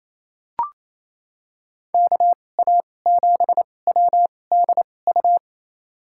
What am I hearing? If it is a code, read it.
Morse code, heard as a keyed tone, about 28 wpm, spaced as for 20 wpm, 700 Hz. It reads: KA7WDU
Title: USA Call Signs - 6 Characters - 2x Character Spacing - 28wpm